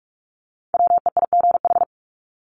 Morse code, heard as a keyed tone, about 44 wpm, 700 Hz. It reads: WEIGH
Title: Top 1751-2000 Words 45wpm